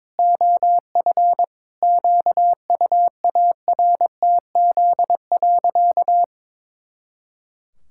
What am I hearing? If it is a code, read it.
Morse code, heard as a keyed tone, about 22 wpm, 700 Hz. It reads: OF QUARTZ.
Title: Pangram Challenge v2 - 22wpm